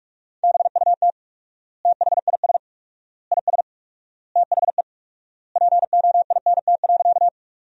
Morse code, heard as a keyed tone, about 45 wpm, 700 Hz. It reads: BUT THIS IS THE POINT.
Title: 4x Word-Spacing - Sentences from Top 200 Words - 45wpm